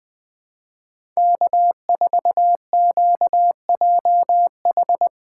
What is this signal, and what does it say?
Morse code, heard as a keyed tone, about 20 wpm, 700 Hz. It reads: K4QJH
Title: USA Call Signs - 5 Characters - 3x Character Spacing - 20wpm